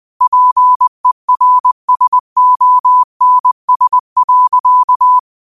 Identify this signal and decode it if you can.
Morse code, heard as a keyed tone, about 20 wpm, 1000 Hz. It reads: PERSONS.
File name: qotdfiles / 20wpm